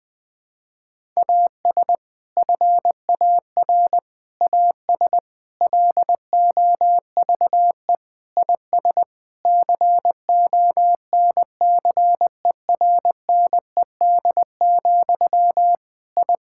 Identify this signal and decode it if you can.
Morse code, heard as a keyed tone, about 20 wpm, 700 Hz. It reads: AS FAR AS LOVE IS CONCERNED, I
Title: Sentences from Top 1000 Words - Review Part 6 - 20wpm